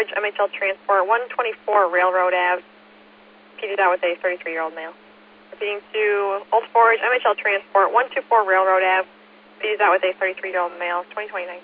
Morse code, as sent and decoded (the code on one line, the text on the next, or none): none